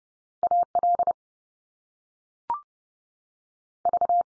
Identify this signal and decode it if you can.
Morse code, heard as a keyed tone, about 30 wpm, 700 Hz. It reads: AL V
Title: US States 30WPM